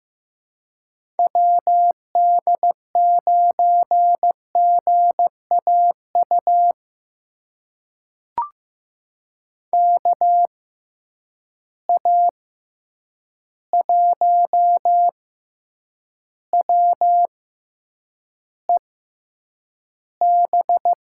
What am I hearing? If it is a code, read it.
Morse code, heard as a keyed tone, about 15 wpm, spaced as for 8 wpm, 700 Hz. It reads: WD9GAU KA1WEB
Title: USA Call Signs - 6 Characters - 6x Character Spacing - 15wpm